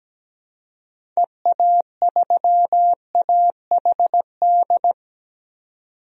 Morse code, heard as a keyed tone, about 17 wpm, 700 Hz. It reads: EA3AHD
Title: DXCC-100 Call Signs - 6 Characters - 17wpm